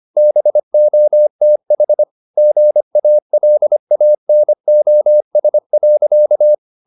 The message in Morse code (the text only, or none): BOTH GALANOS.